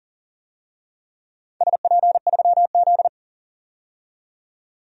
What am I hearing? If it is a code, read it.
Morse code, heard as a keyed tone, about 40 wpm, 700 Hz. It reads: SP3Z
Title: DXCC-100 Call Signs - 4 Characters - 40wpm